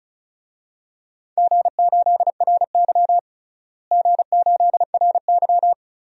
Morse code, heard as a keyed tone, about 36 wpm, 700 Hz. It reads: G8RY G8RY
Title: DXCC-100 Call Signs - 4 Characters - Repeated 2x - 35wpm